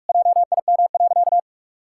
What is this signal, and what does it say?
Morse code, heard as a keyed tone, about 41 wpm, 700 Hz. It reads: JIM.
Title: Pangram Challenge v2 - 45wpm